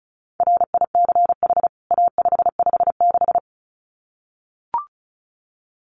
Morse code, heard as a keyed tone, about 34 wpm, 700 Hz. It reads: RICH A556
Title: CWT Exchange 35wpm